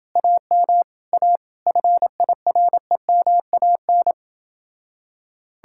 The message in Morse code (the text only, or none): AM A FIREMAN